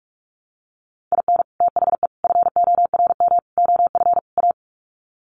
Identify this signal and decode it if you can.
Morse code, heard as a keyed tone, about 44 wpm, 700 Hz. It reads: IN THE FORM OF A
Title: Sets of 5 Words 45wpm